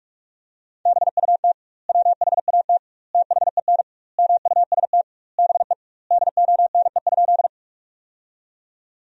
Morse code, heard as a keyed tone, about 42 wpm, 700 Hz. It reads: BUT WHAT THEN MUST BE DONE?
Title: 2x Word-Spacing - Sentences from Top 500 Words - 45wpm